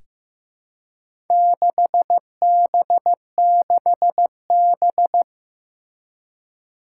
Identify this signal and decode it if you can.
Morse code, heard as a keyed tone, about 15 wpm, 700 Hz. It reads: 6B6B
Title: Four Challenging Letter-Number Combinations 15wpm